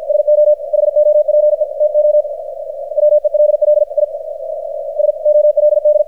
Morse code, heard as a keyed tone, about 53 wpm, 600 Hz, with radio noise. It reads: SO HOPE TO WELL I SOLD
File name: archived / CRSnet